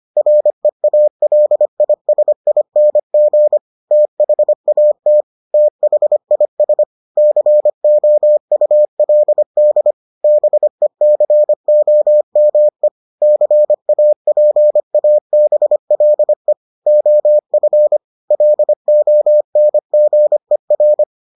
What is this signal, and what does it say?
Morse code, heard as a keyed tone, about 25 wpm, 600 Hz. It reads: REALISING THAT THIS COULD BECOME CAPABLE OF LONGER